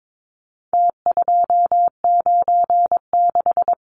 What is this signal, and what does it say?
Morse code, heard as a keyed tone, about 22 wpm, 700 Hz. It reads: T296
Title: Numbers - Contest Style - 4-Digit 22wpm